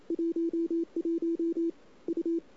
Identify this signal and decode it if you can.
Morse code, heard as a keyed tone, about 28 wpm, 330 Hz. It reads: 11 U
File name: CW / 2008-ss-cw-k5tr-op-n5rz -01:00